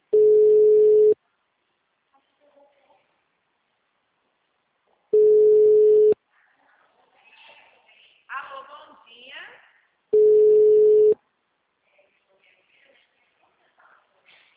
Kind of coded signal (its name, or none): telephone call progress tone